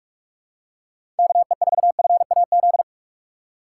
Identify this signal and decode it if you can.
Morse code, heard as a keyed tone, about 44 wpm, 700 Hz. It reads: KE4FAZ
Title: USA Call Signs - 6 Characters - 6x Character Spacing - 45wpm